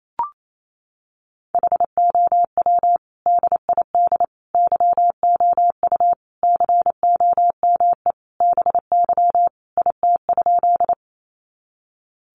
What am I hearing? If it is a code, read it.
Morse code, heard as a keyed tone, about 28 wpm, 700 Hz. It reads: HOW DID YOU COME BY IT?
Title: Sentences from Top 100 Words 28wpm